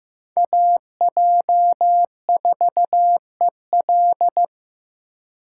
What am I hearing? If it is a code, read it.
Morse code, heard as a keyed tone, about 15 wpm, 700 Hz. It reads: AJ4EL
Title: USA Call Signs - 5 Characters - 3x Character Spacing - 15wpm